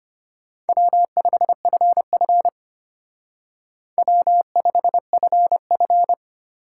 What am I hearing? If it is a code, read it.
Morse code, heard as a keyed tone, about 30 wpm, 700 Hz. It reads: W5FF W5FF
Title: Speed Racing - USA 1x2 Call Signs 25wpm